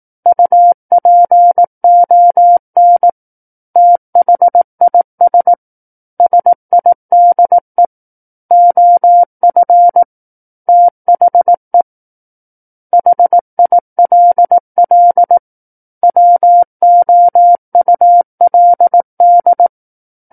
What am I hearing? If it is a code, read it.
Morse code, heard as a keyed tone, about 18 wpm, 700 Hz. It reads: UPON THIS SIDE OF THE HILL WOULD
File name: 18 WPM